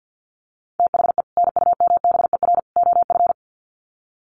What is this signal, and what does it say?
Morse code, heard as a keyed tone, about 50 wpm, 700 Hz. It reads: THE NUMBER OF
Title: Sets of 3 Words 50wpm